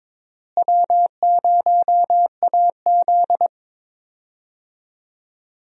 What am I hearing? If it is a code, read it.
Morse code, heard as a keyed tone, about 22 wpm, 700 Hz. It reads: W0AZ
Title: USA Call Signs - 4 Characters - Repeated 3x - 22wpm